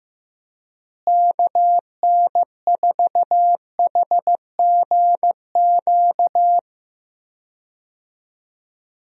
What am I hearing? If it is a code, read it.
Morse code, heard as a keyed tone, about 15 wpm, 700 Hz. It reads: KN4HGQ
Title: USA Call Signs - 6 Characters - Repeated 3x - 15wpm